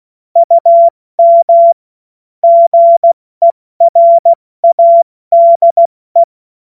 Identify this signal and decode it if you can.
Morse code, heard as a keyed tone, about 16 wpm, 680 Hz. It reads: UM GERADE
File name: Morsekurs / Witze / Witz 07. Dezember 2021, 70 F80